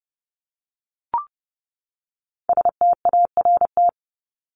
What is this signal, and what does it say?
Morse code, heard as a keyed tone, about 30 wpm, 700 Hz. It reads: START
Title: ED Suffix - Top 100 Words with Base Words 30wpm